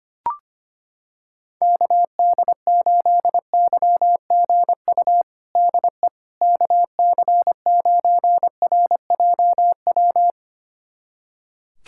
Morse code, heard as a keyed tone, about 25 wpm, 700 Hz. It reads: KD8YGU DE KC9RJW
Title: USA Callsign DE Callsign - 6 Characters - 25wpm